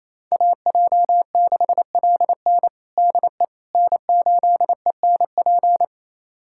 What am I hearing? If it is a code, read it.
Morse code, heard as a keyed tone, about 28 wpm, 700 Hz. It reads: AJ6LN DE N8ENP
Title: USA Callsign DE Callsign - 5 Characters - 28wpm